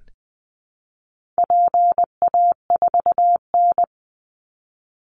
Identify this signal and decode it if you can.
Morse code, heard as a keyed tone, about 20 wpm, 700 Hz. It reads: PA4N
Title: Call Signs Repeated 2x - 20wpm